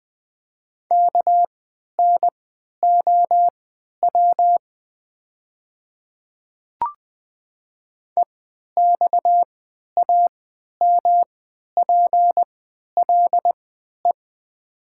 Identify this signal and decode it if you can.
Morse code, heard as a keyed tone, about 20 wpm, spaced as for 11 wpm, 700 Hz. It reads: KNOW EXAMPLE
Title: 3x Character Spacing - Top 500 Words - 20wpm